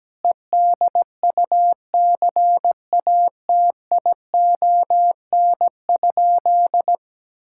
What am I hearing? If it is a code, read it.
Morse code, heard as a keyed tone, about 17 wpm, 700 Hz. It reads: EDUCATION?
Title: Sentences from Top 1000 Words - Review Part 6 - 17wpm